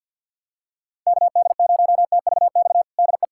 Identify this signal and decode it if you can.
Morse code, heard as a keyed tone, about 51 wpm, 700 Hz. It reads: KD0TVX DE
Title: USA Callsign DE Callsign 50wpm